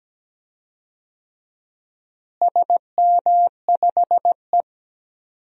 Morse code, heard as a keyed tone, about 17 wpm, 700 Hz. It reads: SM5E
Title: DXCC-100 Call Signs - 4 Characters - Repeated 2x - 17wpm